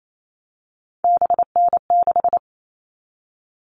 Morse code, heard as a keyed tone, about 28 wpm, 700 Hz. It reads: BN6
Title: All Cut Numbers - 3-Digit 28wpm